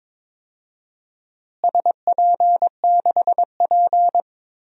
Morse code, heard as a keyed tone, about 22 wpm, 700 Hz. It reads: SP6P
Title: DXCC-100 Call Signs - 4 Characters - Repeated 2x - 22wpm